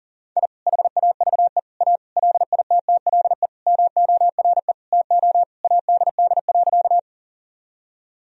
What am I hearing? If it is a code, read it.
Morse code, heard as a keyed tone, about 40 wpm, 700 Hz. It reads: I HAVE A LITTLE MORE TO ADD.
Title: Sentences from Top 1000 Words - Review Part 1 - 40wpm